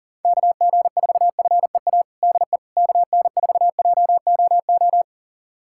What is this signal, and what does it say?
Morse code, heard as a keyed tone, about 40 wpm, 700 Hz. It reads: KG4FEA DE KN4JOO
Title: USA Callsign DE Callsign - 6 Characters - 40wpm